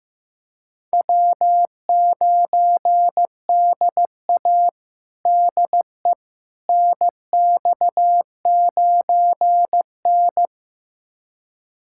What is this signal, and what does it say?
Morse code, heard as a keyed tone, about 15 wpm, 700 Hz. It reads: W9DA DE NX9N
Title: USA Callsign DE Callsign - 4 Characters - 15wpm